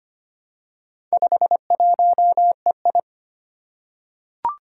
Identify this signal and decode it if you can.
Morse code, heard as a keyed tone, about 25 wpm, 700 Hz. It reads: 51EI
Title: Four Challenging Letter-Number Combinations 25wpm